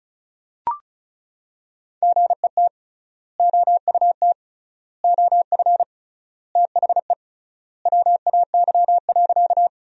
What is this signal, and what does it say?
Morse code, heard as a keyed tone, about 36 wpm, 700 Hz. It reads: GET OUT OF THE WAY.
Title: 3x Word-Spacing - Sentences from Top 200 Words - 35wpm